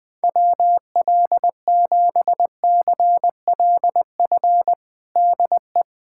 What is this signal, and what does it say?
Morse code, heard as a keyed tone, about 20 wpm, 700 Hz. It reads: WL7CLF DE
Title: USA Callsign DE Callsign - 6 Characters - 20wpm